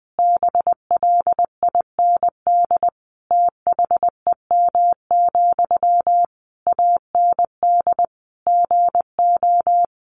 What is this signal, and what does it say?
Morse code, heard as a keyed tone, about 20 wpm, 700 Hz. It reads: BLIND THEM, AND GO
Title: A Sound Of Thunder 20wpm